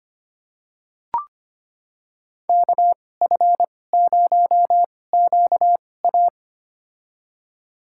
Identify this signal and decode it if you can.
Morse code, heard as a keyed tone, about 25 wpm, spaced as for 18 wpm, 700 Hz. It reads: KF0QA
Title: USA Call Signs - 5 Characters - 2x Character Spacing - 25wpm